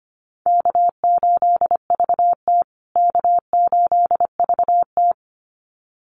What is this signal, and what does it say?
Morse code, heard as a keyed tone, about 25 wpm, 700 Hz. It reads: K8VT K8VT
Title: USA Call Signs Repeated 2x 25wpm